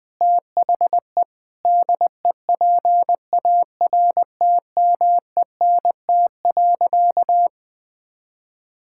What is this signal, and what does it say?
Morse code, heard as a keyed tone, about 20 wpm, 700 Hz. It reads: THE DEPARTMENT.
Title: Sentences with 3-Word Phrases - Spoken Beginning - 20wpm